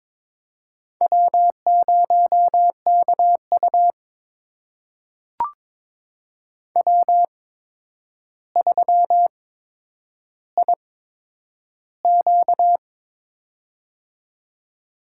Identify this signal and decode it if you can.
Morse code, heard as a keyed tone, about 22 wpm, spaced as for 9 wpm, 700 Hz. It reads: W0KU W3IQ
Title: USA Call Signs - 4 Characters - 8x Character Spacing - 22wpm